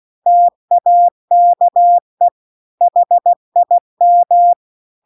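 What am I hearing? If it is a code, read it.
Morse code, heard as a keyed tone, about 16 wpm, 700 Hz. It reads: TAKE HIM
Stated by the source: Morse practice text at 16 WPM.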